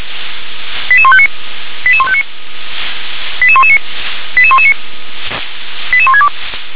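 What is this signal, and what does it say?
Five-tone selective call (ZVEI), heard as quick runs of five tones, tones of about 70 ms. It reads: 88159 88159 88119 88119 88162
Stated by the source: ZVEI five-tone call received by SDR.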